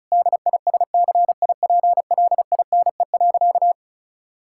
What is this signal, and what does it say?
Morse code, heard as a keyed tone, about 34 wpm, 700 Hz. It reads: DISCIPLINE.
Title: Sentences from Top 1000 Words - Review Part 6 - 35wpm